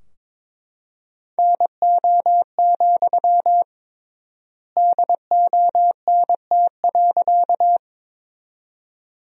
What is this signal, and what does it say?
Morse code, heard as a keyed tone, about 22 wpm, 700 Hz. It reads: NO, DONT.
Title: 3x Word-Spacing - Sentences from Top 300 Words - 22wpm